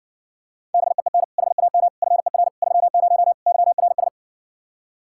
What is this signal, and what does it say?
Morse code, heard as a keyed tone, about 60 wpm, 700 Hz. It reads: BEEN HAM FER 30 YRS